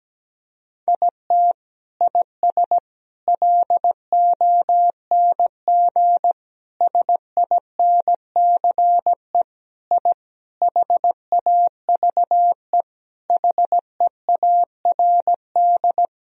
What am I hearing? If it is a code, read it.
Morse code, heard as a keyed tone, about 17 wpm, 700 Hz. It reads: IT IS LONG SINCE I HAVE HEARD
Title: Sentences from Top 1000 Words - Review Part 1 - 17wpm